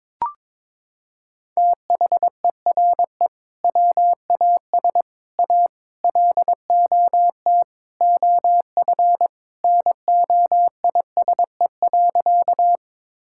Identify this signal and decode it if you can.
Morse code, heard as a keyed tone, about 22 wpm, 700 Hz. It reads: THERE WAS A LOT OF NOISE.
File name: practice / 22wpm / Sentences with 4-Word Phrases - 22wpm